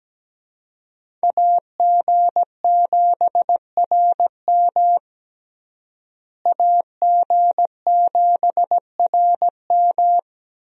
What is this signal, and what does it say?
Morse code, heard as a keyed tone, about 17 wpm, 700 Hz. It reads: AG7RM AG7RM
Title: USA Call Signs - 5 Characters - Repeated 3x - 17wpm